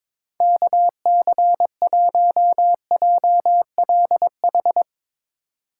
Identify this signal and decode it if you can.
Morse code, heard as a keyed tone, about 23 wpm, 700 Hz. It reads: KC1JLH
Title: USA Call Signs - 6 Characters - 3x Character Spacing - 22wpm